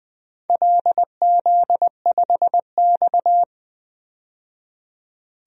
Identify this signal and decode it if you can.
Morse code, heard as a keyed tone, about 20 wpm, 700 Hz. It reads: LZ5X
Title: DXCC-100 Call Signs - 4 Characters - 20wpm